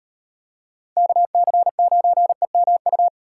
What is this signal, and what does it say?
Morse code, heard as a keyed tone, about 38 wpm, 700 Hz. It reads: KC9EMU